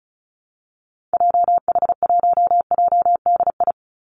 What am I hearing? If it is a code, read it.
Morse code, heard as a keyed tone, about 34 wpm, 700 Hz. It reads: JH1JDI
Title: Call Signs Repeated 2x - 35wpm